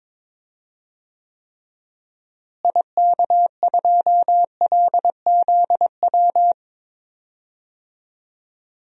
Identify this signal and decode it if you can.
Morse code, heard as a keyed tone, about 22 wpm, 700 Hz. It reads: IK2LZW
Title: DXCC-100 Call Signs - 6 Characters - Repeated 2x - 22wpm